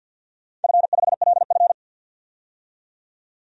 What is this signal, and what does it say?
Morse code, heard as a keyed tone, about 50 wpm, 700 Hz. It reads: F5LF
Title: DXCC-100 Call Signs - 4 Characters - 50wpm